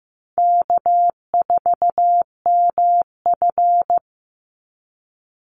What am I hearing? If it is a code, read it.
Morse code, heard as a keyed tone, about 15 wpm, 700 Hz. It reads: K4MF
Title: NA SOTA Chaser Callsigns 15wpm